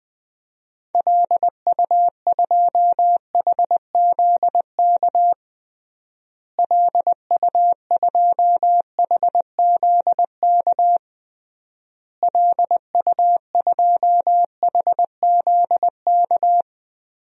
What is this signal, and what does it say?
Morse code, heard as a keyed tone, about 21 wpm, 700 Hz. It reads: LU2HZK LU2HZK LU2HZK
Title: DXCC-100 Call Signs - 6 Characters - Repeated 3x - 20wpm